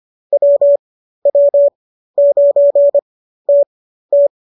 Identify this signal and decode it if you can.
Morse code, heard as a keyed tone, about 25 wpm, spaced as for 13 wpm, 570 Hz. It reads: WW9TT